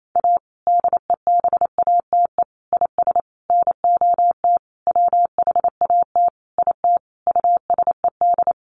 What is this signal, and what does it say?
Morse code, heard as a keyed tone, about 28 wpm, 700 Hz. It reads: A DEBATE IS NOT WHAT IT USED
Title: Sentences from Top 1100 Words 28wpm